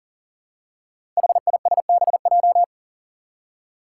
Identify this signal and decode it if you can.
Morse code, heard as a keyed tone, about 40 wpm, 700 Hz. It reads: HISBJ
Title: Five Challenging Letter-Number Combinations 40wpm